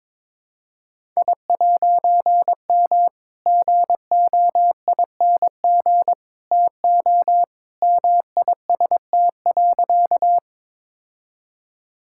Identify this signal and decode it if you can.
Morse code, heard as a keyed tone, about 22 wpm, 700 Hz. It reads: I'M GOING TO MIST.
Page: morse-code-ninja-practice-22wpm